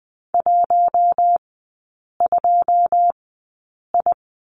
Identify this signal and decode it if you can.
Morse code, heard as a keyed tone, about 19 wpm, spaced as for 8 wpm, 700 Hz. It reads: 12I
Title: 087 Lesson 15e - Number 5 - Three Characters 20wpm